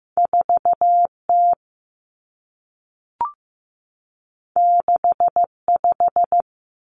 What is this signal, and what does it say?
Morse code, heard as a keyed tone, about 15 wpm, 700 Hz. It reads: 4T 65
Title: Common Cut Numbers - 2-Digit 15wpm